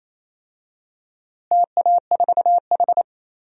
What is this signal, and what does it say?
Morse code, heard as a keyed tone, about 28 wpm, 700 Hz. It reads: TA4H
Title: DXCC-100 Call Signs - 4 Characters - Repeated 3x - 28wpm